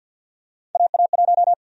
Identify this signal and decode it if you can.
Morse code, heard as a keyed tone, about 53 wpm, 700 Hz. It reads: AA1
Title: USA Call Sign Prefixes - 50wpm